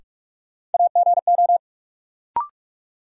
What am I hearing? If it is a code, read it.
Morse code, heard as a keyed tone, about 46 wpm, 700 Hz. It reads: AGO
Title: Sets of 2 Words - New Word Drill - 45wpm